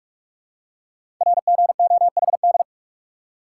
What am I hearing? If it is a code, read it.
Morse code, heard as a keyed tone, about 44 wpm, 700 Hz. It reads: RGOHD
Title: Code Groups - 5 Letters - 45wpm